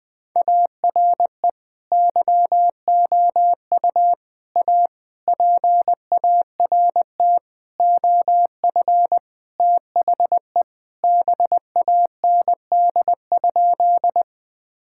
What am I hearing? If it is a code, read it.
Morse code, heard as a keyed tone, about 21 wpm, 700 Hz. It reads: ARE YOU A PART OF THE BAND?
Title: Sentences from Top 1600 Words 20wpm